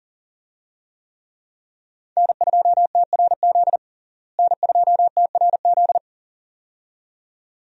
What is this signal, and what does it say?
Morse code, heard as a keyed tone, about 40 wpm, 700 Hz. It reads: N2TRZ N2TRZ
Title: USA Call Signs - 5 Characters - Repeated 2x - 40wpm